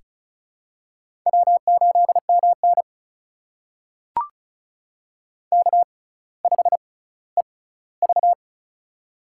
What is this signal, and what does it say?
Morse code, heard as a keyed tone, about 36 wpm, spaced as for 18 wpm, 700 Hz. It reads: W8MN K5EV
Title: USA Call Signs - 4 Characters - 6x Character Spacing - 35wpm